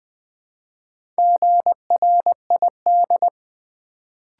Morse code, heard as a keyed tone, about 20 wpm, 700 Hz. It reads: GRID
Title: Build A Word - 1 Letter At A Time - 3 to 5-Letter Words - 20wpm